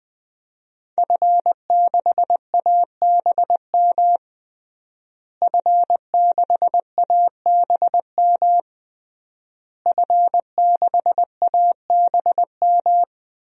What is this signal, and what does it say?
Morse code, heard as a keyed tone, about 20 wpm, 700 Hz. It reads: F6ABM F6ABM F6ABM
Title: DXCC-100 Call Signs - 5 Characters - Repeated 3x - 20wpm